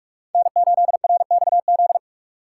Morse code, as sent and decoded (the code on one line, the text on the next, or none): -. ---.. .-. -..- --..
N8RXZ